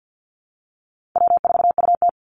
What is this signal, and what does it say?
Morse code, heard as a keyed tone, about 49 wpm, 700 Hz. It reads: W4UT
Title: Call Signs Repeated 3x - 50wpm